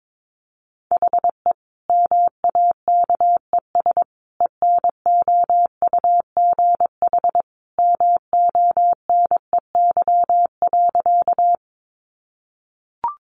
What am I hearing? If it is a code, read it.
Morse code, heard as a keyed tone, about 22 wpm, 700 Hz. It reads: HE MAKES ENOUGH MONEY.
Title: Additional Easy Phrases 22wpm